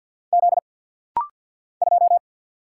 Morse code, heard as a keyed tone, about 50 wpm, 700 Hz. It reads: Z 2
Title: Single Letter-Number Alt - Rapid-Fire 50wpm